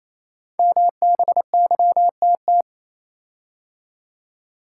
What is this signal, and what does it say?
Morse code, heard as a keyed tone, about 27 wpm, 700 Hz. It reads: MBYTT
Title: Code Groups - 5 Letter:Numbers - 28wpm